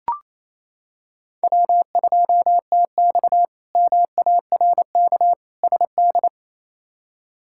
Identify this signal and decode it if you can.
Morse code, heard as a keyed tone, about 27 wpm, 700 Hz. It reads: W2TX MARK SD
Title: K1USN SST QSO Practice - 28wpm